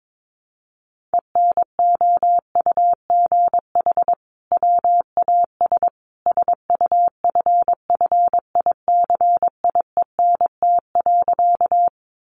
Morse code, heard as a keyed tone, about 22 wpm, 700 Hz. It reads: ENOUGH WAS SUFFICIENT.
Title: Sentences from Top 1400 Words 22wpm